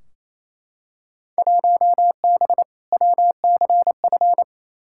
Morse code, heard as a keyed tone, about 28 wpm, 700 Hz. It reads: 1B WCF
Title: Field Day Exchanges Encore - 28wpm